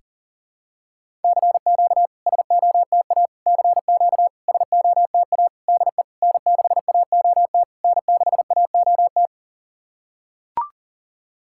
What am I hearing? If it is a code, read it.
Morse code, heard as a keyed tone, about 40 wpm, 690 Hz. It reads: CQ SOTA CQ SOTA DE N6AOT N6AOT
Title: SOTA QSO Practice 40wpm